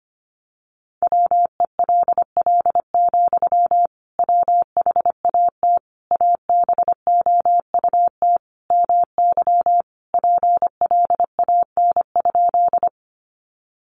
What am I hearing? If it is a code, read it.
Morse code, heard as a keyed tone, about 25 wpm, 700 Hz. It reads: WELL, WHAT ABOUT MY PLAN?
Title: Sentences from Top 500 Words 25wpm